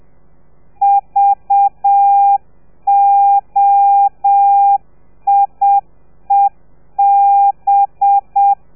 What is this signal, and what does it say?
Morse code, heard as a keyed tone, about 7 wpm, 800 Hz. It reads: VOIEB
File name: telegrafia / i2tfj / Lez4 30